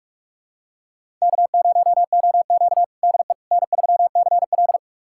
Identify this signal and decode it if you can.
Morse code, heard as a keyed tone, about 46 wpm, 700 Hz. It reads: K0OQ DE N3CL
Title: USA Callsign DE Callsign - 4 Characters - 45wpm